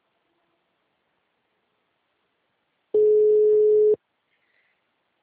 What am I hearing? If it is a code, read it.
Ringback tone (Europe, 425 Hz).